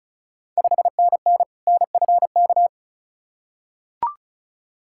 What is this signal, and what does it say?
Morse code, heard as a keyed tone, about 36 wpm, 700 Hz. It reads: 5NN NFK
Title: Ontario QSO Party Exchange 35wpm